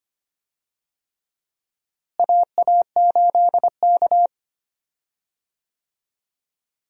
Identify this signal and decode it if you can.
Morse code, heard as a keyed tone, about 25 wpm, 700 Hz. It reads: AA8K
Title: USA Call Signs - 4 Characters - Repeated 3x - 25wpm